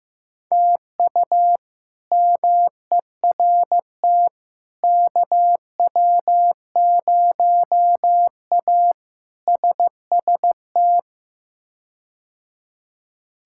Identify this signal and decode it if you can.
Morse code, heard as a keyed tone, about 16 wpm, 700 Hz. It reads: TU MERT KW0A SST
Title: K1USN SST QSO Practice - 15wpm